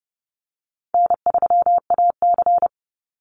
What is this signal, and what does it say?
Morse code, heard as a keyed tone, about 30 wpm, 700 Hz. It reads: N3AC
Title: Easy Call Signs Repeated 2x - 30wpm